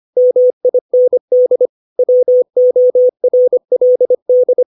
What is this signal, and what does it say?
Morse code, heard as a keyed tone, about 25 wpm, 500 Hz. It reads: MIND WORLD
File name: 632 words / 632 Words 25WPM